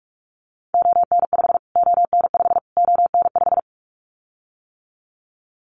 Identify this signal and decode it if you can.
Morse code, heard as a keyed tone, about 44 wpm, 700 Hz. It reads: ON5 ON5 ON5
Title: Callsign Prefixes for Top 25 DXCC - Repeated 3x - 45wpm